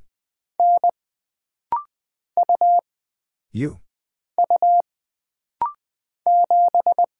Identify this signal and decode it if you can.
Morse code, heard as a keyed tone, about 20 wpm, 700 Hz. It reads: N U U 7
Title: Single Character (Letter:Number:Punctuation:Prosign) - Rapid-Fire - 20wpm